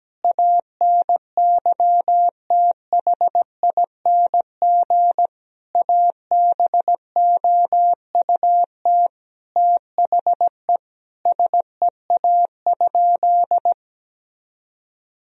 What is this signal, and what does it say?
Morse code, heard as a keyed tone, about 17 wpm, 700 Hz. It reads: ANYTHING ABOUT THE SEA?
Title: Sentences from Top 1000 Words - Review Part 1 - 17wpm